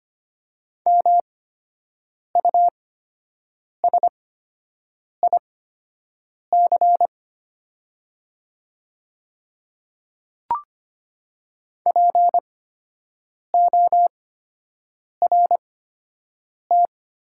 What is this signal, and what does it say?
Morse code, heard as a keyed tone, about 25 wpm, spaced as for 7 wpm, 690 Hz. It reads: MUSIC PORT
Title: 8x Character Spacing - Top 500 Words - 25wpm